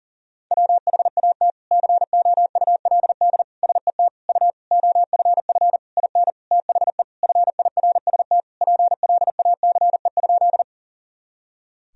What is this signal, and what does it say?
Morse code, heard as a keyed tone, about 40 wpm, 700 Hz. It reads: WHAT COULD SET U OFF IN THE FIRST PLACE?
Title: Sentences from Top 200 Words and Abbreviations 40wpm